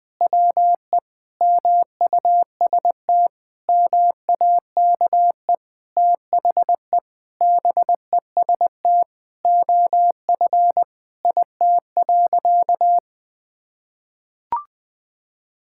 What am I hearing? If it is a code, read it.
Morse code, heard as a keyed tone, about 20 wpm, 700 Hz. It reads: WE MUST MAKE THE BEST OF IT.
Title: Sentences from Top 1000 Words - Review Part 5 - 20wpm